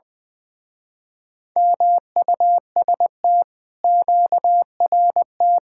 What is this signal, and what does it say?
Morse code, heard as a keyed tone, about 22 wpm, 700 Hz. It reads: MUST QRT
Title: Speed-Racing Common QSO Phrases Encore - 20wpm